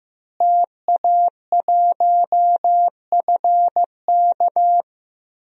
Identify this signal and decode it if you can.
Morse code, heard as a keyed tone, about 15 wpm, 700 Hz. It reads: TA1FK
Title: DXCC-100 Call Signs - 5 Characters - Repeated 3x - 15wpm